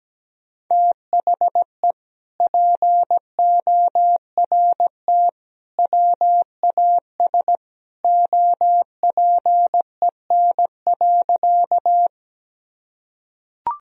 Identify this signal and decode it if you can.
Morse code, heard as a keyed tone, about 17 wpm, 700 Hz. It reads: THE PORT WAS OPEN.
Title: Sentences from Top 1000 Words - Review Part 1 - 17wpm